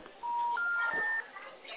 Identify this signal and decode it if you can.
Special information tone, with speech.